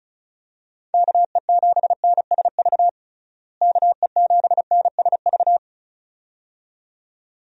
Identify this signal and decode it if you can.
Morse code, heard as a keyed tone, about 34 wpm, 700 Hz. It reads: KE7NSV KE7NSV
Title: USA Call Signs - 6 Characters - Repeated 2x - 35wpm